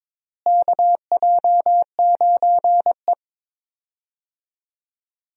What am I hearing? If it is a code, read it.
Morse code, heard as a keyed tone, about 22 wpm, 700 Hz. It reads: KJ9E
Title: USA Call Signs - 4 Characters - Repeated 3x - 22wpm